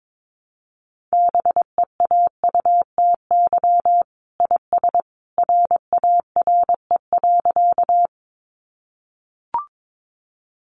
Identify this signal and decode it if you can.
Morse code, heard as a keyed tone, about 22 wpm, 700 Hz. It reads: BEAUTY IS RARE.
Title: Sentences from Top 1100 Words 22wpm